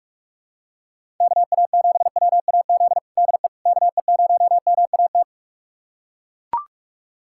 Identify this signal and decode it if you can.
Morse code, heard as a keyed tone, about 41 wpm, 700 Hz. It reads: KA7WAZ DE KE0MAT
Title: USA Callsign DE Callsign - 6 Characters - 45wpm